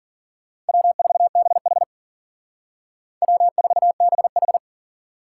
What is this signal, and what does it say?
Morse code, heard as a keyed tone, about 51 wpm, 700 Hz. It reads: W4BH W4BH